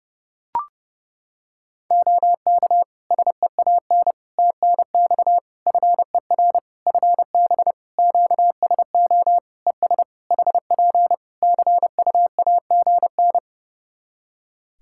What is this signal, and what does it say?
Morse code, heard as a keyed tone, about 31 wpm, 700 Hz. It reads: OK SEAN TNX FER FB QSO ES HP CUAGN
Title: Practice QSO - Exchange 4 - Ending - 30wpm